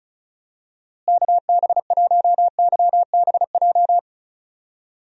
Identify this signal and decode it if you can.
Morse code, heard as a keyed tone, about 36 wpm, 700 Hz. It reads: KB1YBJ
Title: USA Call Signs - 6 Characters - Repeated 2x - 35wpm